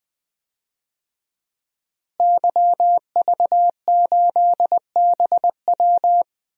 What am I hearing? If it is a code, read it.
Morse code, heard as a keyed tone, about 20 wpm, 700 Hz. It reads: YV8BW
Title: DXCC-100 Call Signs - 5 Characters - Repeated 2x - 20wpm